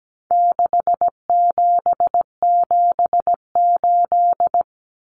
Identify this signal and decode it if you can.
Morse code, heard as a keyed tone, about 17 wpm, 700 Hz. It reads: 6778
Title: Numbers - 4-Digit 17wpm